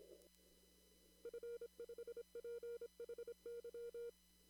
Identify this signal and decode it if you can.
Morse code, heard as a keyed tone, about 25 wpm, 470 Hz. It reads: F5PHY
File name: CW / kt5j wpx cw-op-k5pi